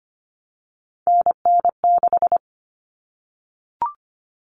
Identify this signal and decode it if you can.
Morse code, heard as a keyed tone, about 25 wpm, 700 Hz. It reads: NN6
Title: Common Cut Numbers - 3-Digit 25wpm